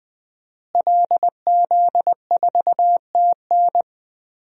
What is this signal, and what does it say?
Morse code, heard as a keyed tone, about 21 wpm, 700 Hz. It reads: LZ4TN
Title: DXCC-100 Call Signs - 5 Characters - Repeated 3x - 20wpm